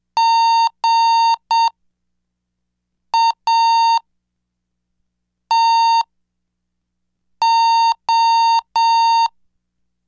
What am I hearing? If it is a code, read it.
Morse code, heard as a keyed tone, about 7 wpm, spaced as for 4 wpm, 920 Hz. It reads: GATO
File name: cursos / CDestudo / 11 Track 11